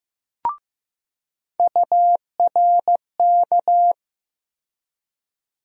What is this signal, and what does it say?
Morse code, heard as a keyed tone, about 15 wpm, 700 Hz. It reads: URK
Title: USA Call Sign Postfixes - 3 Letters 15wpm